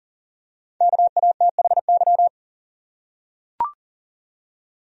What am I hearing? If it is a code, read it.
Morse code, heard as a keyed tone, about 40 wpm, 700 Hz. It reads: KATHY
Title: USA Common Names - Max 5 Letters - 40wpm